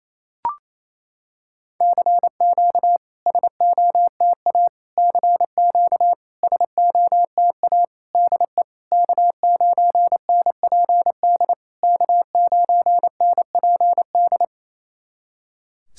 Morse code, heard as a keyed tone, about 28 wpm, 690 Hz. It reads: CQ SOTA CQ SOTA DE K9NPD K9NPD
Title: SOTA QSO Practice 28wpm